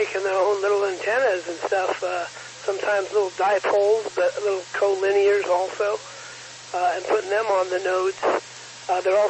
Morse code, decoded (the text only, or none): none